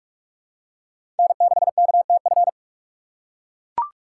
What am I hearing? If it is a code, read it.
Morse code, heard as a keyed tone, about 42 wpm, 700 Hz. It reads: N6KTF
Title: USA Call Signs - 6x Character Spacing - 45wpm